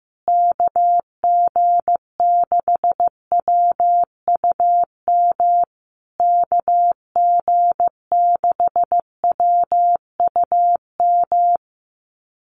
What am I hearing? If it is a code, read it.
Morse code, heard as a keyed tone, about 15 wpm, 700 Hz. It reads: KG6WUM KG6WUM
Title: USA Call Signs Repeated 3x 15wpm